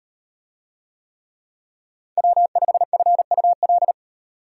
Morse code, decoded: W5FUL

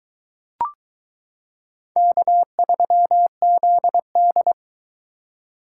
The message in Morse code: K3ZD

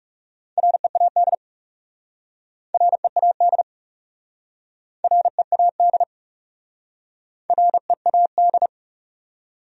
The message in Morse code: READ READ READ READ

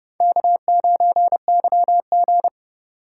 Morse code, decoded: K9YG

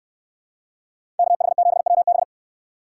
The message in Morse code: KH7WB

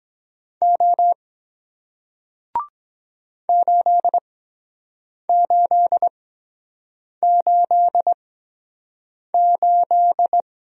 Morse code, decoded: O 8888

